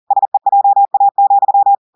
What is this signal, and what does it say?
Morse code, heard as a keyed tone, about 40 wpm, 800 Hz. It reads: SEJA,
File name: audios / portugais / portugais texte en clair 2 vitesse 40 mots minutes